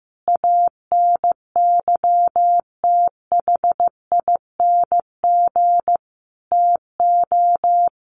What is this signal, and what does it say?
Morse code, heard as a keyed tone, about 15 wpm, 700 Hz. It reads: ANYTHING TO
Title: Sentences from Top 1000 Words 15wpm